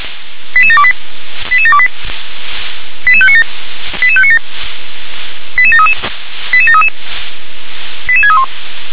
Five-tone selective call (ZVEI), heard as quick runs of five tones, tones of about 70 ms. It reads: 88728 88728 88597 88597 88733 88733 88631
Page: SDR/ZVEI/88730